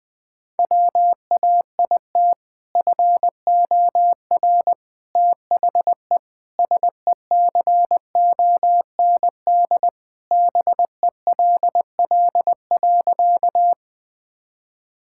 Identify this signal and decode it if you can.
Morse code, heard as a keyed tone, about 20 wpm, 700 Hz. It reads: WAIT FOR THE SECOND BELL.
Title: Sentences with 2-Word Phrases - 20wpm